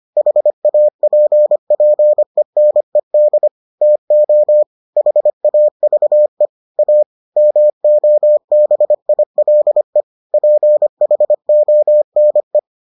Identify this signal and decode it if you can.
Morse code, heard as a keyed tone, about 25 wpm, 600 Hz. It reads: HAPPENED TO HAVE A MOBILE PHONE